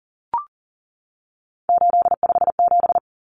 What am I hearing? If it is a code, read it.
Morse code, heard as a keyed tone, about 40 wpm, 700 Hz. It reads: 857
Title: Common Cut Numbers - 3-Digit 40wpm